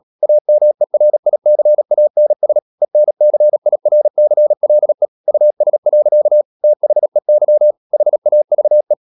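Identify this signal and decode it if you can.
Morse code, heard as a keyed tone, about 37 wpm, 600 Hz. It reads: AMERICANS ENCIRCLE US. THEY HAVE
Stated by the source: Morse code practice at 35–39 wpm sending English text of Ken Follett's novel Never, from CW med Gnister.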